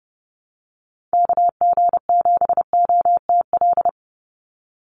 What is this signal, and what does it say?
Morse code, heard as a keyed tone, about 30 wpm, 700 Hz. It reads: KG7OTL
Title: USA Call Signs 30wpm